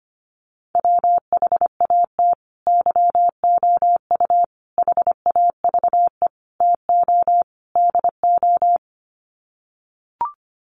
Morse code, heard as a keyed tone, about 25 wpm, 700 Hz. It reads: WHAT YOU HAVE TO DO
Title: Sets of 5 Words Encore 25wpm